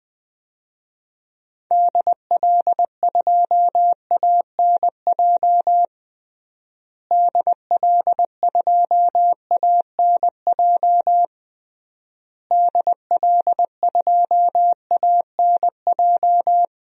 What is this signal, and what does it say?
Morse code, heard as a keyed tone, about 20 wpm, 700 Hz. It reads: DL2ANJ DL2ANJ DL2ANJ
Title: DXCC-100 Call Signs - 6 Characters - Repeated 3x - 20wpm